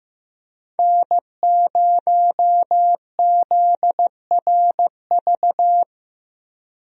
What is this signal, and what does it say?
Morse code, heard as a keyed tone, about 16 wpm, 700 Hz. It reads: N0ZRV
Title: USA Call Signs - 5 Characters - Repeated 3x - 15wpm